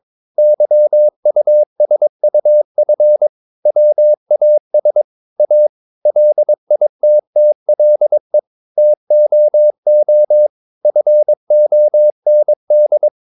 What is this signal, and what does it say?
Morse code, heard as a keyed tone, about 22 wpm, 600 Hz. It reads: YUSUF WAS A LITTLE TOO FOND
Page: Romanen Never af Ken Follet på engelsk.